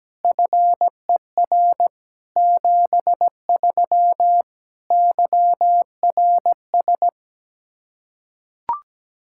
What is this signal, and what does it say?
Morse code, heard as a keyed tone, about 17 wpm, 700 Hz. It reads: FER 73 YRS
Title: Practice QSO - Exchange 3 - More Info - 17wpm